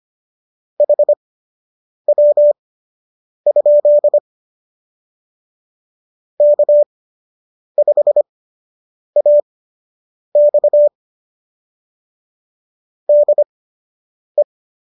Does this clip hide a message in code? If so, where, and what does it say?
Yes: Morse code from 0.8 to 14.4 s, heard as a keyed tone, about 25 wpm, spaced as for 8 wpm, 590 Hz. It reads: HW? K5AX DE